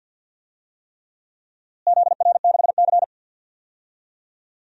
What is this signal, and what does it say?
Morse code, heard as a keyed tone, about 47 wpm, 700 Hz. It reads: ZR6C
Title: DXCC-100 Call Signs - 4 Characters - Repeated 3x - 50wpm